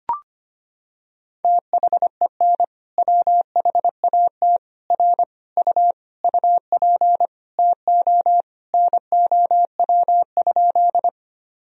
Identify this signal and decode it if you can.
Morse code, heard as a keyed tone, about 25 wpm, 700 Hz. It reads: THEN WHAT R U UP TO NOW?
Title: Sentences from Top 100 Words and Abbreviations 25wpm